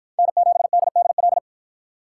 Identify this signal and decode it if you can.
Morse code, heard as a keyed tone, about 53 wpm, 700 Hz. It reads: N7DDL